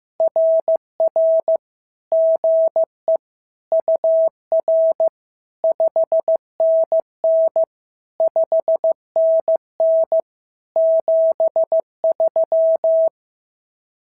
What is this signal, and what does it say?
Morse code, heard as a keyed tone, about 15 wpm, 650 Hz. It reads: RR GE UR 5NN 5NN 73
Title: SOTA QSO Practice 15wpm